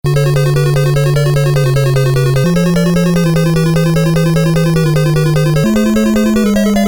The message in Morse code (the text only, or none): none